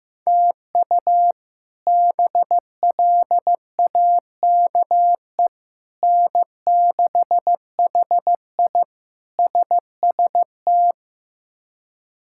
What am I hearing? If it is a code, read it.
Morse code, heard as a keyed tone, about 15 wpm, 700 Hz. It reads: TU BLAKE N6HI SST
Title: K1USN SST QSO Practice - 15wpm